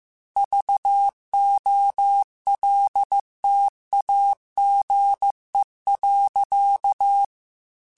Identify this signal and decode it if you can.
Morse code, heard as a keyed tone, about 15 wpm, 800 Hz. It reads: VOLTAGE.
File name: Technical Subjects / Technical Subjects-mp3s / XL @13